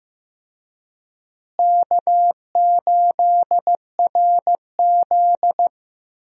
Morse code, heard as a keyed tone, about 16 wpm, 700 Hz. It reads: K8RZ
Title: USA Call Signs - 4 Characters - Repeated 3x - 15wpm